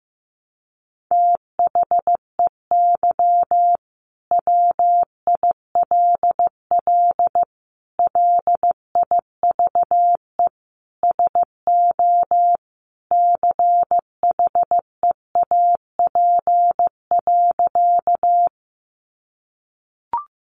Morse code, heard as a keyed tone, about 14 wpm, 700 Hz. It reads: THEY WILL LIVE SO CHEAP.
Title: Sentences from Top 1200 Words 15wpm